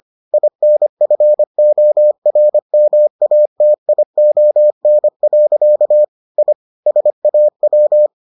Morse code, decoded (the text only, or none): INFORMATION. I SAW